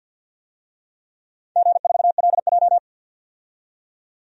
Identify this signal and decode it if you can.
Morse code, heard as a keyed tone, about 53 wpm, 700 Hz. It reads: G4LJ
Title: DXCC-100 Call Signs - 4 Characters - 50wpm